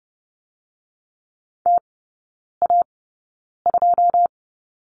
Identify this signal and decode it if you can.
Morse code, heard as a keyed tone, about 30 wpm, spaced as for 9 wpm, 700 Hz. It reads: TA2
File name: course / 30:10wpm / 087 Lesson 15e - Number 5 - Three Characters 30wpm